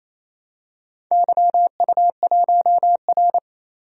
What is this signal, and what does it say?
Morse code, heard as a keyed tone, about 27 wpm, 700 Hz. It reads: YU1R
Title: DXCC-100 Call Signs - 4 Characters - 28wpm